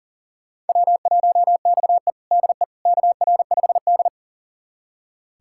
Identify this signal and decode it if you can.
Morse code, heard as a keyed tone, about 37 wpm, 700 Hz. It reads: W1XE DE KR5D
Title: USA Callsign DE Callsign - 4 Characters - 40wpm